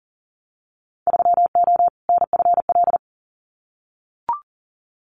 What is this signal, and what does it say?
Morse code, heard as a keyed tone, about 40 wpm, 700 Hz. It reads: 3O NFL
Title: Winter Field Day Exchanges 40wpm